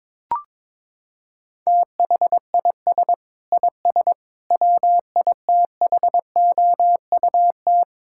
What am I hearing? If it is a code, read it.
Morse code, heard as a keyed tone, about 22 wpm, 700 Hz. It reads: THIS IS WITHOUT
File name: Sentences with 7-Word Phrases - 22wpm